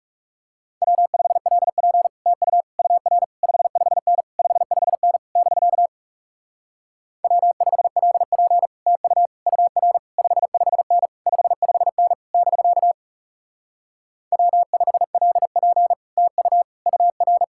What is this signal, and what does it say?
Morse code, heard as a keyed tone, about 40 wpm, 700 Hz. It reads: W5LP TU UR 55N 55N <BK> W5LP TU UR 55N 55N <BK> W5LP TU UR